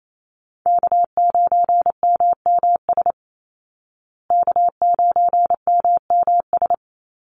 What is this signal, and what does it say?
Morse code, heard as a keyed tone, about 28 wpm, 700 Hz. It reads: K9MMS K9MMS
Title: Call Signs Repeated 2x - 28wpm